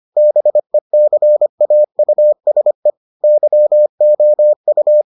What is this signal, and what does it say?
Morse code, heard as a keyed tone, about 25 wpm, 600 Hz. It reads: BECAUSE YOU